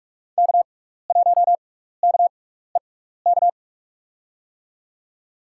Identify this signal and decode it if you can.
Morse code, heard as a keyed tone, about 46 wpm, spaced as for 16 wpm, 700 Hz. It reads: K1KEK
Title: USA Call Signs - 5 Characters - 6x Character Spacing - 45wpm